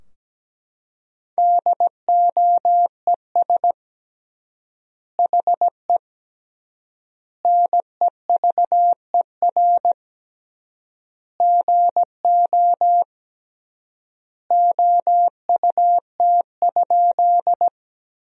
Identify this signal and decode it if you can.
Morse code, heard as a keyed tone, about 17 wpm, 700 Hz. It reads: DOES HE NEVER GO OUT?
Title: 3x Word-Spacing - Sentences from Top 300 Words - 17wpm